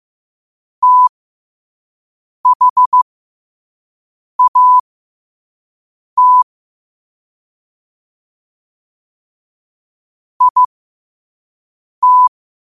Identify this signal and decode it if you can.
Morse code, heard as a keyed tone, about 15 wpm, spaced as for 5 wpm, 1000 Hz. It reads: THAT IT
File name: qotdfiles / 5wpm